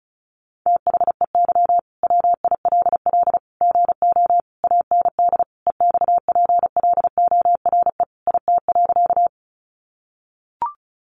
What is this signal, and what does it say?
Morse code, heard as a keyed tone, about 34 wpm, 700 Hz. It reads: THEY WILL GO AND EXPLORE IT.
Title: Sentences from Top 1100 Words 35wpm